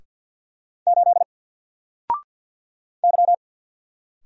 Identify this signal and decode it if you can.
Morse code, heard as a keyed tone, about 48 wpm, 700 Hz. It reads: 8 Y
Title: Single Letter-Number Alt 50wpm